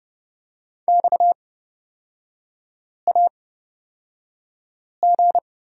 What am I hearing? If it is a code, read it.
Morse code, heard as a keyed tone, about 31 wpm, spaced as for 5 wpm, 700 Hz. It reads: XAG
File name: Lesson 37e - Letter x - Three Characters 30wpm